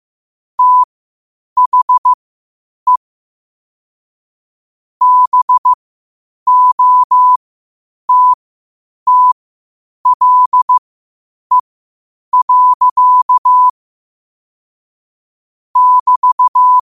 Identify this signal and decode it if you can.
Morse code, heard as a keyed tone, about 15 wpm, spaced as for 8 wpm, 1000 Hz. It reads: THE BOTTLE. <BT>
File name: qotdfiles / 8wpm